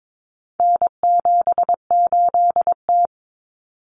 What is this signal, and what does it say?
Morse code, heard as a keyed tone, about 22 wpm, 700 Hz. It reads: N78T
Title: Common Cut Numbers - 4-Digit 22wpm